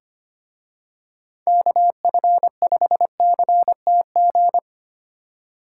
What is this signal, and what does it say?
Morse code, heard as a keyed tone, about 28 wpm, 700 Hz. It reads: KF5CTG
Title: Speed Racing - USA 2x3 Call Signs 25wpm